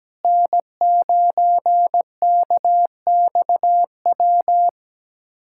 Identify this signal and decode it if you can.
Morse code, heard as a keyed tone, about 17 wpm, 700 Hz. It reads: N9KXW
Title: USA Callsign DE Callsign - 5 Characters - 17wpm